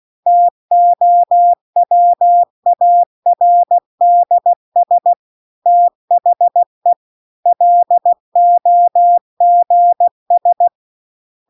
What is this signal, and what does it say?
Morse code, heard as a keyed tone, about 16 wpm, 700 Hz. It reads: TOWARDS THE LOGS
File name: CW-The-Big-Twoheaded-River-2-16WPM